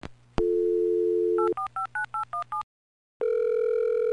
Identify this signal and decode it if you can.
DTMF, heard as telephone touch tones, tones of about 120 ms. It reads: 8459817